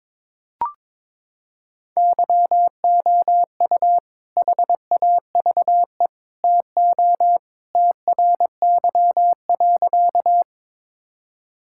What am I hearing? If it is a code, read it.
Morse code, heard as a keyed tone, about 23 wpm, 700 Hz. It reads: YOU HAVE TO TRY.
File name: practice / 22wpm / Sentences with 3-Word Phrases - 22wpm